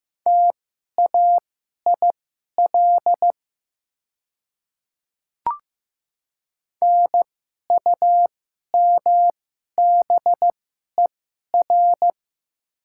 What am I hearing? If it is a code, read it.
Morse code, heard as a keyed tone, about 15 wpm, spaced as for 11 wpm, 690 Hz. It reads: TAIL NUMBER
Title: 2x Character Spacing - Top 500 Words - 15wpm